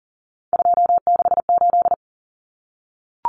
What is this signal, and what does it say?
Morse code, heard as a keyed tone, about 37 wpm, 700 Hz. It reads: 268
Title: Common Cut Numbers - 3-Digit 40wpm